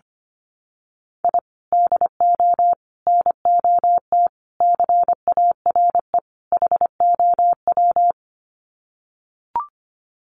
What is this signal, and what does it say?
Morse code, heard as a keyed tone, about 25 wpm, 700 Hz. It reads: I DO NOT CARE HOW
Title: Sets of 5 Words Encore2 25wpm